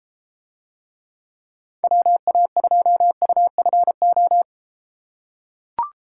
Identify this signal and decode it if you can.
Morse code, heard as a keyed tone, about 32 wpm, 700 Hz. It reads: WA2UFO